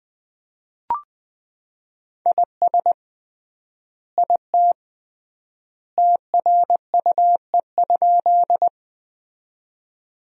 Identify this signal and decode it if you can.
Morse code, heard as a keyed tone, about 19 wpm, 700 Hz. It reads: IS IT TRUE?
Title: 3x Word-Spacing - Sentences from Top 400 Words - 20wpm